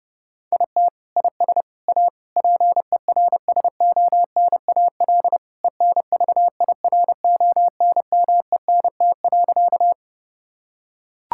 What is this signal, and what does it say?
Morse code, heard as a keyed tone, about 30 wpm, 700 Hz. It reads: IT IS A PERSONAL ENVIRONMENT.
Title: Sentences from Top 1000 Words - Review Part 5 - 30wpm